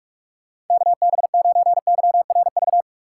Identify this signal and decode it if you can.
Morse code, heard as a keyed tone, about 46 wpm, 700 Hz. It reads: KB9YRV
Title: USA Call Signs - 6 Characters - 2x Character Spacing - 45wpm